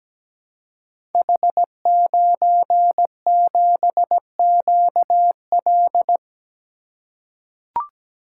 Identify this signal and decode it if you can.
Morse code, heard as a keyed tone, about 17 wpm, 700 Hz. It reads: H97QL
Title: Code Groups - 5 Letter:Numbers - 17wpm